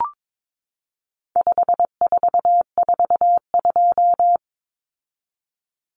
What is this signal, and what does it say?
Morse code, heard as a keyed tone, about 22 wpm, 700 Hz. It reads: 5442
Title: Numbers - 4-Digit 22wpm